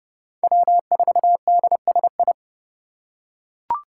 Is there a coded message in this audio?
Yes: Morse code from 0.4 to 2.3 s, heard as a keyed tone, about 27 wpm, 700 Hz. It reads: W4DSI